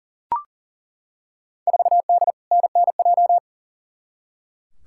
Morse code, heard as a keyed tone, about 40 wpm, 700 Hz. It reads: 4D NNJ
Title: Field Day Exchanges Encore - 40wpm